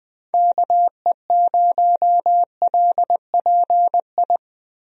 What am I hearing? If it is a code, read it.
Morse code, heard as a keyed tone, about 20 wpm, 700 Hz. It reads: KE0LPI
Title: USA Call Signs - 6 Characters - Repeated 3x - 20wpm